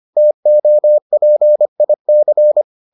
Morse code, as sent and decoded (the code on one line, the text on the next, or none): - --- .--. .. -.-.
TOPIC